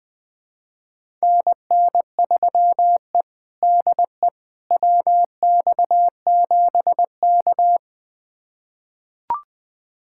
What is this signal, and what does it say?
Morse code, heard as a keyed tone, about 20 wpm, 700 Hz. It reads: NN3E DE WX7K
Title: USA Callsign DE Callsign - 4 Characters - 20wpm